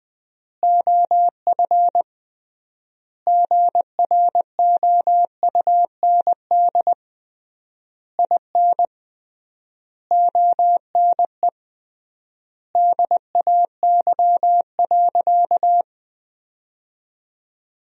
Morse code, heard as a keyed tone, about 20 wpm, 700 Hz. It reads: OF GROUND IN ONE DAY.
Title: 3x Word-Spacing - Sentences from Top 500 Words - 20wpm